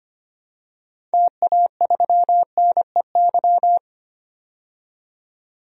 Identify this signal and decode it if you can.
Morse code, heard as a keyed tone, about 25 wpm, 700 Hz. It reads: TA3NEY
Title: DXCC-100 Call Signs - 6 Characters - 25wpm